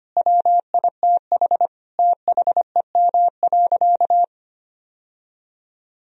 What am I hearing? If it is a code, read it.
Morse code, heard as a keyed tone, about 25 wpm, 700 Hz. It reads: WITH THEM.
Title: Sentences from Top 1000 Words - Review Part 4 - 25wpm